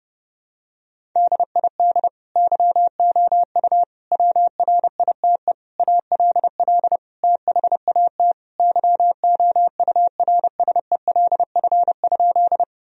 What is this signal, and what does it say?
Morse code, heard as a keyed tone, about 30 wpm, 700 Hz. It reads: DID YOU WRITE ALL THAT YOURSELF?
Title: Sentences from Top 1000 Words - Review Part 1 - 30wpm